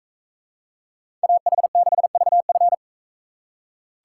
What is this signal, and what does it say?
Morse code, heard as a keyed tone, about 41 wpm, 700 Hz. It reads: AH6VF